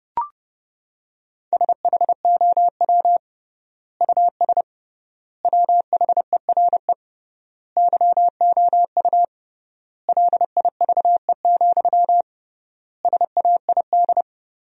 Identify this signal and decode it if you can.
Morse code, heard as a keyed tone, about 30 wpm, 700 Hz. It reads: SHOW US WHERE YOU LIVE, SAID
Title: 3x Word-Spacing - Sentences from Top 200 Words - 30wpm